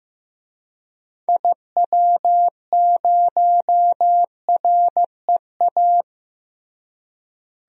Morse code, heard as a keyed tone, about 15 wpm, 700 Hz. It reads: IW0REA
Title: DXCC-100 Call Signs - 6 Characters - Repeated 2x - 15wpm